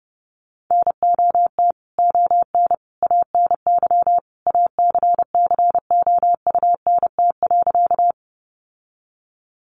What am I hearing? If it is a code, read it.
Morse code, heard as a keyed tone, about 30 wpm, 700 Hz. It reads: NOT ON ANY ACCOUNT.
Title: Sentences from Top 900 Words 30wpm